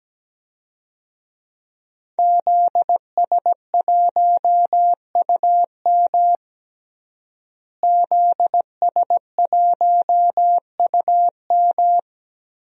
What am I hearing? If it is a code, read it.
Morse code, heard as a keyed tone, about 17 wpm, 700 Hz. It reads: ZS1UM ZS1UM
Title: DXCC-100 Call Signs - 5 Characters - Repeated 3x - 17wpm